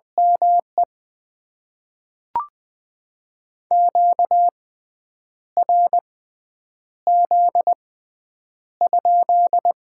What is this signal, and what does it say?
Morse code, heard as a keyed tone, about 20 wpm, spaced as for 8 wpm, 700 Hz. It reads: ME QRZ?
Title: Top 90 QSO Elements - 6x Character Spacing - 20wpm